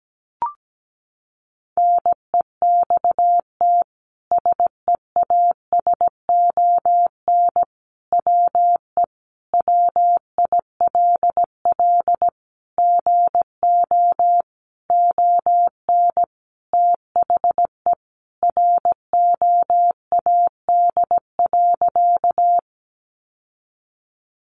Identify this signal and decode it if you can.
Morse code, heard as a keyed tone, about 17 wpm, 700 Hz. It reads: NEXT SEASON WE WILL GO ON THE ROAD.
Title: Sentences from Top 1400 Words 17wpm